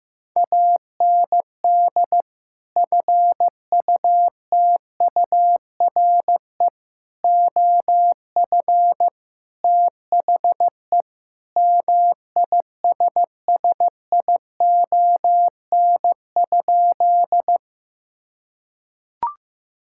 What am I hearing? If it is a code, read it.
Morse code, heard as a keyed tone, about 15 wpm, 700 Hz. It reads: AND FUTURE OF THE MISSION?
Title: Sentences from Top 1600 Words 15wpm